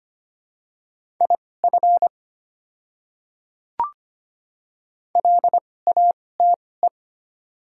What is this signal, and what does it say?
Morse code, heard as a keyed tone, about 25 wpm, spaced as for 18 wpm, 700 Hz. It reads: IF LATE
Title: 2x Character Spacing - Top 300 Words - 25wpm